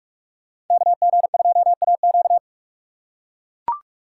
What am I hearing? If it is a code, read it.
Morse code, heard as a keyed tone, about 42 wpm, 700 Hz. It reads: KG2AQ